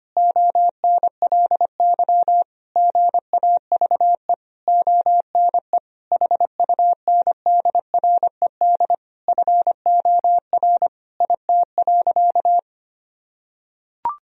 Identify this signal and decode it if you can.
Morse code, heard as a keyed tone, about 25 wpm, 700 Hz. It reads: ONLY GAVE ONE HUNDRED FOR IT.
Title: Sentences from Top 1000 Words - Review Part 2 - 25wpm